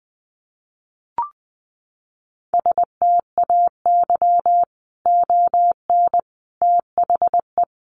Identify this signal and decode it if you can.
Morse code, heard as a keyed tone, about 20 wpm, 700 Hz. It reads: STAY ON THE
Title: A Sound Of Thunder 20wpm